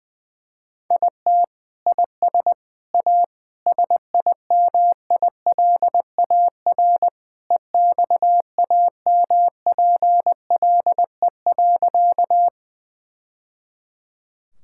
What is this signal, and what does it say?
Morse code, heard as a keyed tone, about 20 wpm, 700 Hz. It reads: IT IS A SIMILAR EXAMPLE.
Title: Sentences from Top 1000 Words - Review Part 5 - 20wpm